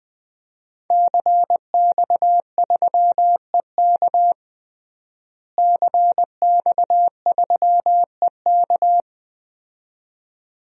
Morse code, heard as a keyed tone, about 20 wpm, 700 Hz. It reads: CX3EK CX3EK
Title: DXCC-100 Call Signs - 5 Characters - Repeated 2x - 20wpm